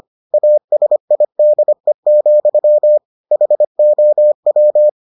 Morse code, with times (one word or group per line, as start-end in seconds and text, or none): 0.3-3.0 ASIDE,
3.3-4.9 HOW